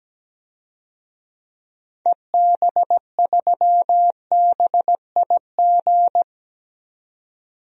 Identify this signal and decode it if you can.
Morse code, heard as a keyed tone, about 17 wpm, 700 Hz. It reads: EB3BIG
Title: DXCC-100 Call Signs - 6 Characters - 17wpm